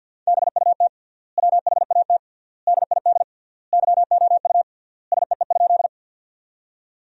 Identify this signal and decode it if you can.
Morse code, heard as a keyed tone, about 50 wpm, 700 Hz. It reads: BUT WHAT DID YOU SEE?
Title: 3x Word-Spacing - Sentences from Top 100 Words - 50wpm